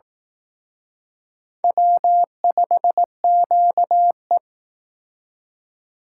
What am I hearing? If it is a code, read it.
Morse code, heard as a keyed tone, about 18 wpm, 700 Hz. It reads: W5QE
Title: Speed Racing - USA 1x2 Call Signs 15wpm